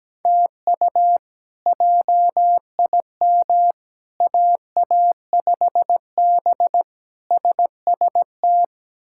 Morse code, heard as a keyed tone, about 17 wpm, 700 Hz. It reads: TU JIM AA5B SST
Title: K1USN SST QSO Practice - 17wpm